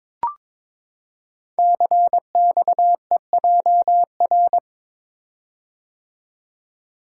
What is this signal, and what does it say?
Morse code, heard as a keyed tone, about 22 wpm, 690 Hz. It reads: CXEJR
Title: Code Groups - 5 Letter:Numbers - 22wpm